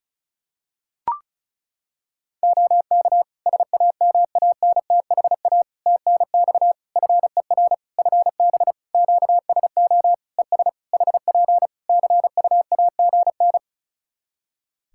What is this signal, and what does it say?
Morse code, heard as a keyed tone, about 36 wpm, 700 Hz. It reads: OK SAMANTHA TNX FER FB QSO ES HP CUAGN
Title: Practice QSO - Exchange 4 - Ending - 35wpm